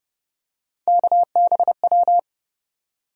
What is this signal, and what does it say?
Morse code, heard as a keyed tone, about 32 wpm, 700 Hz. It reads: KBW
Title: USA Call Sign Postfixes - 3 Letters 30wpm